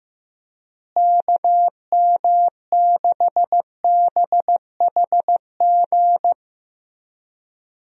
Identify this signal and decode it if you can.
Morse code, heard as a keyed tone, about 15 wpm, 700 Hz. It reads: KM6BHG
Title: USA Call Signs - 6 Characters - Repeated 3x - 15wpm